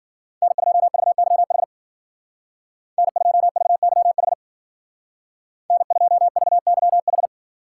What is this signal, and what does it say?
Morse code, heard as a keyed tone, about 50 wpm, 700 Hz. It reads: N2VYH N2VYH N2VYH
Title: Speed Racing - USA 1x3 Call Signs 40wpm